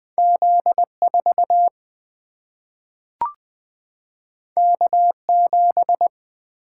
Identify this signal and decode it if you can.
Morse code, heard as a keyed tone, about 20 wpm, 700 Hz. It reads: Z4 K7
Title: Single-Letter Single-Number 20wpm